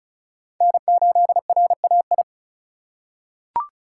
Morse code, heard as a keyed tone, about 34 wpm, 700 Hz. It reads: N8RAI
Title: USA Call Signs - 5 Characters - 4x Character Spacing - 35wpm